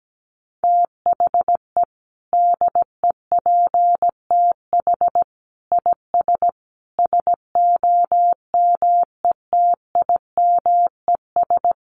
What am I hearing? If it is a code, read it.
Morse code, heard as a keyed tone, about 17 wpm, 700 Hz. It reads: THE DEPTH IS SOMETIMES
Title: Sentences from Top 1200 Words 17wpm